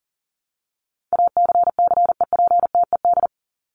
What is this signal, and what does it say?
Morse code, heard as a keyed tone, about 40 wpm, 700 Hz. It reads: ACCEPTED
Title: ED Suffix - Top 100 Words with Base Words 40wpm